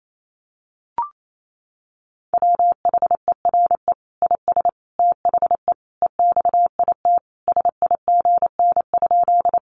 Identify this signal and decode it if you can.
Morse code, heard as a keyed tone, about 28 wpm, 700 Hz. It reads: WHERE IS THE EXIT SIGN?
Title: Sentences from Top 1100 Words 28wpm